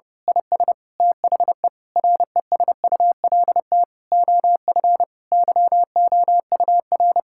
Morse code, heard as a keyed tone, about 30 wpm, 700 Hz. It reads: IS THE RESULT OF YOUR
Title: Sentences from Top 1000 Words - Review Part 5 - 30wpm